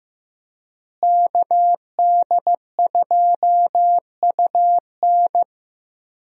Morse code, heard as a keyed tone, about 15 wpm, 700 Hz. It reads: KD2UN
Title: USA Call Signs - 5 Characters - Repeated 3x - 15wpm